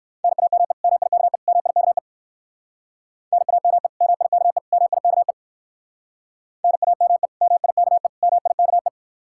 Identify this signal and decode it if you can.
Morse code, heard as a keyed tone, about 56 wpm, 700 Hz. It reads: NAME MIKE MIKE NAME MIKE MIKE NAME MIKE MIKE
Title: Speed-Racing Common QSO Phrases Encore - 45wpm